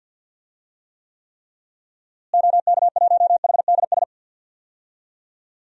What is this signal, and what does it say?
Morse code, heard as a keyed tone, about 50 wpm, 700 Hz. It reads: OK1HDS
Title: DXCC-100 Call Signs - 6 Characters - Repeated 3x - 50wpm